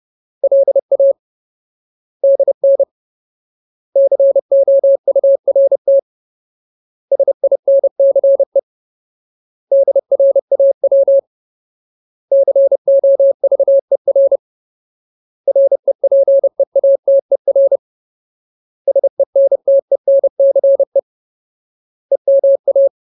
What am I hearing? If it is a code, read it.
Morse code, heard as a keyed tone, about 30 wpm, 550 Hz. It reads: LA DN COURT SINCE DRAW COVER REPEATER SENTENCE EMA
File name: Ultimate Word List / 30WPM Ultimate